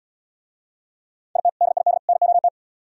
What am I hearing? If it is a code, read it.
Morse code, heard as a keyed tone, about 73 wpm, 700 Hz. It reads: IT DID NOT